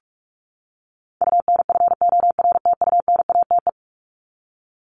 Morse code, heard as a keyed tone, about 44 wpm, 700 Hz. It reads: UNFORTUNATE
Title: LY Suffix - Top 100 Words with Base Words 45wpm